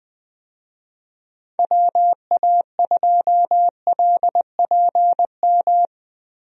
Morse code, heard as a keyed tone, about 21 wpm, 700 Hz. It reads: WA2LPM
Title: USA Call Signs - 6 Characters - Repeated 3x - 20wpm